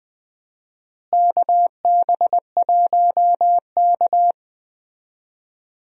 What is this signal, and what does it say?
Morse code, heard as a keyed tone, about 20 wpm, 700 Hz. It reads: KB1K
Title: USA Call Signs - 4 Characters - Repeated 3x - 20wpm